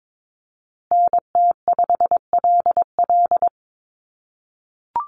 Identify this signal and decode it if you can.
Morse code, heard as a keyed tone, about 22 wpm, 700 Hz. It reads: NT5LL
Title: USA Call Signs Repeated 3x 22wpm